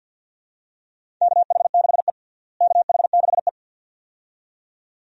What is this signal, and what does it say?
Morse code, heard as a keyed tone, about 51 wpm, 700 Hz. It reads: KH6E KH6E
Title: DXCC-100 Call Signs - 4 Characters - Repeated 2x - 50wpm